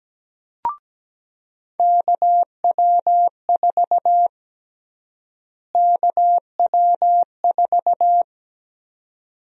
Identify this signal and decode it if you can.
Morse code, heard as a keyed tone, about 17 wpm, 700 Hz. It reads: KW4 KW4
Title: USA Call Sign Prefixes - Repeated 2x - 17wpm